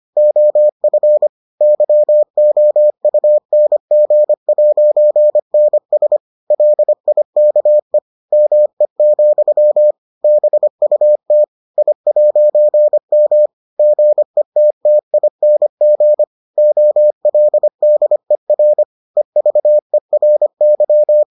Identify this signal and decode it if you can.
Morse code, heard as a keyed tone, about 25 wpm, 600 Hz. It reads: OF YOUNG'NS LIKE ME, BUT I'M GETTING OLDER EVERY